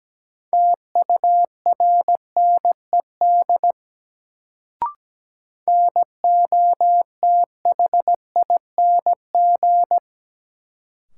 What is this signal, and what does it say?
Morse code, heard as a keyed tone, about 17 wpm, 700 Hz. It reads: TURNED NOTHING
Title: Sets of 4 Words - New Word Drill - 17wpm